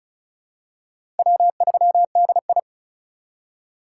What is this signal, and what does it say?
Morse code, heard as a keyed tone, about 34 wpm, 700 Hz. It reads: W3DI
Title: USA Call Signs - 4 Characters - 3x Character Spacing - 35wpm